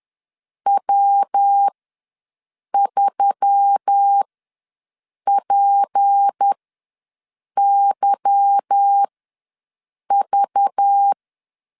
Morse code, heard as a keyed tone, about 10 wpm, spaced as for 6 wpm, 800 Hz. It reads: W3PYV